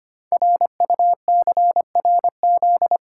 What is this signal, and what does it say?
Morse code, heard as a keyed tone, about 25 wpm, 700 Hz. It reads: RUCRZ
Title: Code Groups - 5 Letters - 25wpm